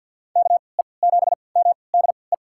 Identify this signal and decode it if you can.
Morse code, heard as a keyed tone, about 48 wpm, spaced as for 28 wpm, 700 Hz. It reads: KE7MDE
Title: USA Call Signs - 6 Characters - 3x Character Spacing - 50wpm